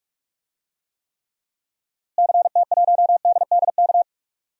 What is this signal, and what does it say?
Morse code, heard as a keyed tone, about 46 wpm, 700 Hz. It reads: CT1DDK
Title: DXCC-100 Call Signs - 6 Characters - Repeated 3x - 45wpm